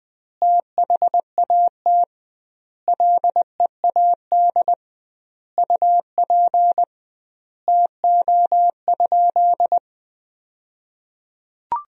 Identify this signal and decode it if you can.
Morse code, heard as a keyed tone, about 21 wpm, 700 Hz. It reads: THAT LEAD UP TO?
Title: 2x Word-Spacing - Sentences from Top 500 Words - 20wpm